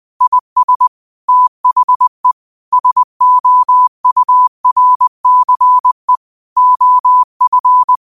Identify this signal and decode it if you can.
Morse code, heard as a keyed tone, about 20 wpm, 1000 Hz. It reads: IS THE SOURCE OF
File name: qotdfiles / 20wpm